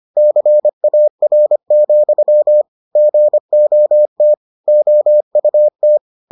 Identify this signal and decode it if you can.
Morse code, heard as a keyed tone, about 25 wpm, 600 Hz. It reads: CAR, GOT OUT